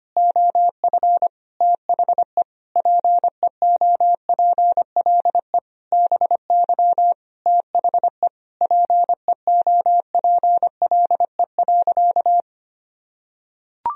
Morse code, heard as a keyed tone, about 25 wpm, 700 Hz. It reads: OF THE PEOPLE BY THE PEOPLE.
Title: Sentences from Top 1000 Words - Review Part 3 - 25wpm